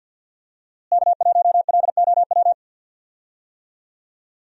Morse code, heard as a keyed tone, about 47 wpm, 700 Hz. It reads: K1LOW
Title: USA Call Signs - 5 Characters - Repeated 3x - 50wpm